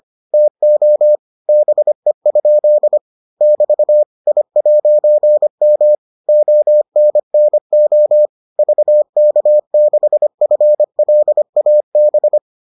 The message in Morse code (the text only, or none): TO BE? <BT> I'M ONNO VK6FLAB